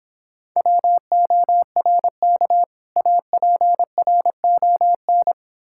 Morse code, heard as a keyed tone, about 26 wpm, 700 Hz. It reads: WORK APRON